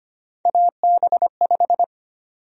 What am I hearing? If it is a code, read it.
Morse code, heard as a keyed tone, about 25 wpm, 700 Hz. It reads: AB5
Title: USA Call Sign Prefixes - 25wpm